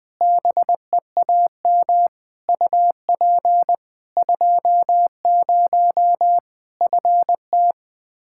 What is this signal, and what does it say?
Morse code, heard as a keyed tone, about 21 wpm, 700 Hz. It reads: BEAM UP 20 FT
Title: Practice QSO - Exchange 2 - Info - 20wpm